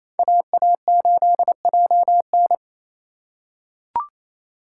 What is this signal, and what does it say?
Morse code, heard as a keyed tone, about 27 wpm, 700 Hz. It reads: AA8JN
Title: USA Call Signs - 5 Characters - 4x Character Spacing - 28wpm